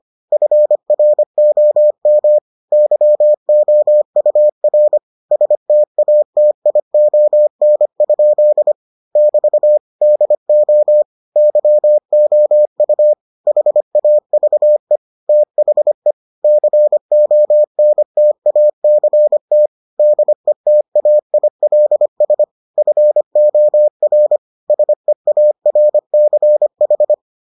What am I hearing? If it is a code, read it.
Morse code, heard as a keyed tone, about 25 wpm, 600 Hz. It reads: FROM YOUR STATION? <BT> DO YOU HAVE THE CONTACT DETAILS FOR SEARCH